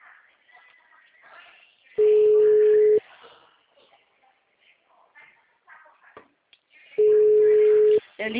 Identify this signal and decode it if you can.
Ringback tone (Europe, 425 Hz), with speech.
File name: out- -2038- - - .10228